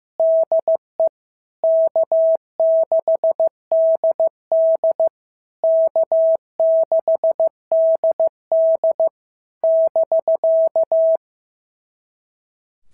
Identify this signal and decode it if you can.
Morse code, heard as a keyed tone, about 14 wpm, 650 Hz. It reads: DE K6DD K6DD <BK>
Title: SOTA QSO Practice 15wpm